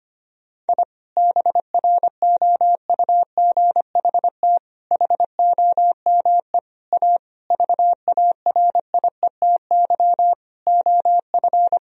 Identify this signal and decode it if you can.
Morse code, heard as a keyed tone, about 25 wpm, 700 Hz. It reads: I BROUGHT HOME A VARIETY OF
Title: Sentences from Top 1000 Words - Review Part 4 - 25wpm